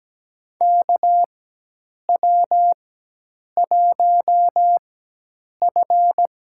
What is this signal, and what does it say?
Morse code, heard as a keyed tone, about 17 wpm, spaced as for 8 wpm, 700 Hz. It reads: KW1F
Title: USA Call Signs - 4 Characters - 4x Character Spacing - 17wpm